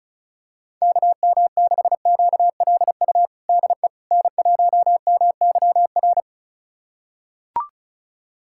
Morse code, heard as a keyed tone, about 36 wpm, 700 Hz. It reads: KM6QLU DE N1MYR
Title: USA Callsign DE Callsign 35wpm